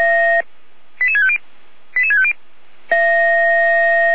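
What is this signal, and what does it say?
Five-tone selective call (ZVEI), heard as quick runs of five tones, tones of about 70 ms. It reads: 88650 88650
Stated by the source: ZVEI five-tone selective call received on SDR.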